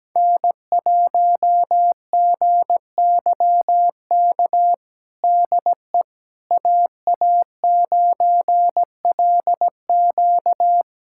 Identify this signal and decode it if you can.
Morse code, heard as a keyed tone, about 17 wpm, 700 Hz. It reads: N1GYK DE AA9LQ
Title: USA Callsign DE Callsign - 5 Characters - 17wpm